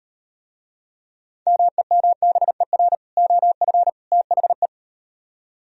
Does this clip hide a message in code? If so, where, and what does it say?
Yes: Morse code from 1.5 to 4.7 s, heard as a keyed tone, about 38 wpm, 700 Hz. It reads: MEMBER OF THE